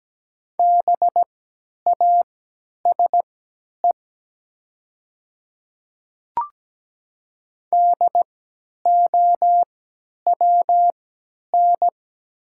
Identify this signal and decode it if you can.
Morse code, heard as a keyed tone, about 17 wpm, spaced as for 10 wpm, 700 Hz. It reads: BASE DOWN
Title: 3x Character Spacing - Top 300 Words - 17wpm